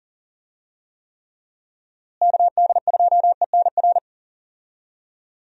Morse code, heard as a keyed tone, about 40 wpm, 700 Hz. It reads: KD2ENR
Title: USA Call Signs - 6 Characters - Repeated 2x - 40wpm